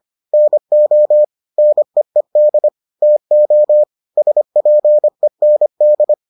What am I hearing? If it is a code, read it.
Morse code, heard as a keyed tone, about 25 wpm, 600 Hz. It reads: NO NEED TO SPEND